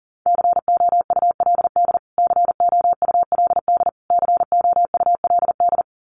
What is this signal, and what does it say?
Morse code, heard as a keyed tone, about 40 wpm, 700 Hz. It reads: COULD COULD COULD
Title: Top 500 Words Repeated 40wpm